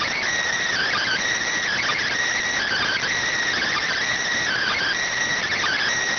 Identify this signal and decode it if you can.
Emergency Alert System (SAME) tones (header bursts), heard as harsh buzzing bursts. It is heard from 0.7 to 5.0 s.
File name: findings / CT AXIAL -572mm 2.gen-38 rx